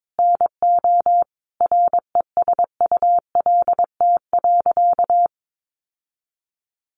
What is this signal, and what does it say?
Morse code, heard as a keyed tone, about 22 wpm, 700 Hz. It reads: NO RESULT.
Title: Sentences from Top 700 Words 22wpm